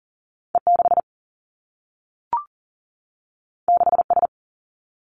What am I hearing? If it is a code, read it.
Morse code, heard as a keyed tone, about 39 wpm, 700 Hz. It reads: E 6 6 S
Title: Two Challenging Letter-Number Combinations 40wpm